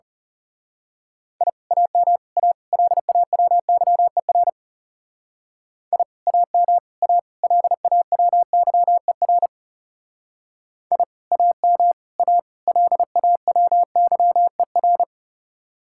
Morse code, heard as a keyed tone, about 34 wpm, 700 Hz. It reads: I AM A LAWYER I AM A LAWYER I AM A LAWYER
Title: Speed-Racing Common QSO Phrases Encore - 30wpm